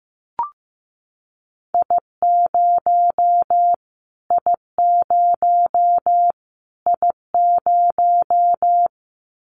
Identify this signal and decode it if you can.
Morse code, heard as a keyed tone, about 15 wpm, 700 Hz. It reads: I0 I0 I0
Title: Callsign Prefixes for Top 25 DXCC - Repeated 3x - 15wpm